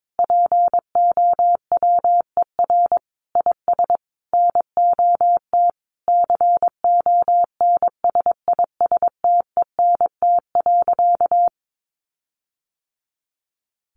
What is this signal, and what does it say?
Morse code, heard as a keyed tone, about 22 wpm, 700 Hz. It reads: POWER IS NOT CONSISTENT.
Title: Sentences from Top 1500 Words 22wpm